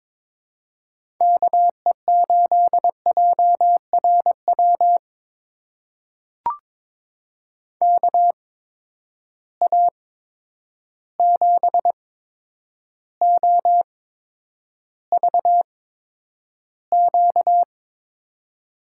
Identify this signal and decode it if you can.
Morse code, heard as a keyed tone, about 23 wpm, spaced as for 9 wpm, 700 Hz. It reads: KE8JRW KA7OVQ
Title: USA Call Signs - 6 Characters - 8x Character Spacing - 22wpm